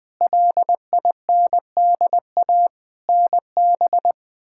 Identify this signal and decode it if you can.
Morse code, heard as a keyed tone, about 21 wpm, 700 Hz. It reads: LINDA NB
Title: K1USN SST QSO Practice - 20wpm